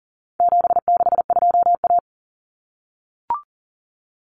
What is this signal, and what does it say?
Morse code, heard as a keyed tone, about 40 wpm, 700 Hz. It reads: 762A
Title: Common Cut Numbers - 4-Digit 40wpm